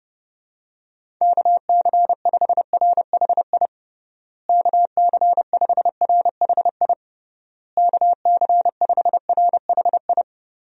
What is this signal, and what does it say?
Morse code, heard as a keyed tone, about 32 wpm, 700 Hz. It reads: KC5RHI KC5RHI KC5RHI
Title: USA Call Signs - 6 Characters - Repeated 3x - 30wpm